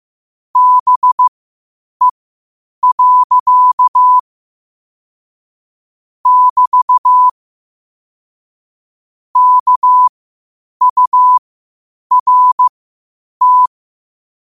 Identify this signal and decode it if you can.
Morse code, heard as a keyed tone, about 15 wpm, spaced as for 8 wpm, 1000 Hz. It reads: BE. <BT> KURT
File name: qotdfiles / 8wpm